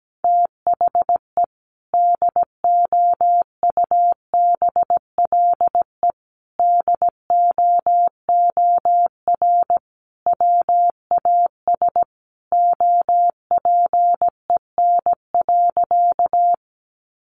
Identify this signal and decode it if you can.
Morse code, heard as a keyed tone, about 17 wpm, 700 Hz. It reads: THE DOUBLE DOOR WAS OPEN.
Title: Sentences from Top 1400 Words 17wpm